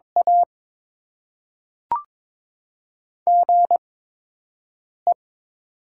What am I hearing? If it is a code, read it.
Morse code, heard as a keyed tone, about 22 wpm, spaced as for 6 wpm, 700 Hz. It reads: A GE
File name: Top 20 QSO Elements - 8x Character Spacing - 22wpm